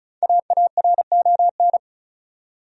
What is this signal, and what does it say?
Morse code, heard as a keyed tone, about 36 wpm, 700 Hz. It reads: AARON
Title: USA Common Names - 35wpm